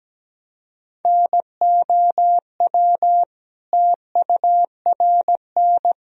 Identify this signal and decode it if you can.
Morse code, heard as a keyed tone, about 17 wpm, 700 Hz. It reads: NOW TURN
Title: Sentences from Top 1000 Words - Review Part 1 - 17wpm